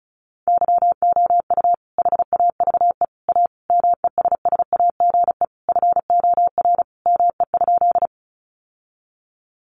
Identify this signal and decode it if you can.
Morse code, heard as a keyed tone, about 34 wpm, 700 Hz. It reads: YOU HAVE A MESSAGE FOR ME?
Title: Sentences from Top 1100 Words 35wpm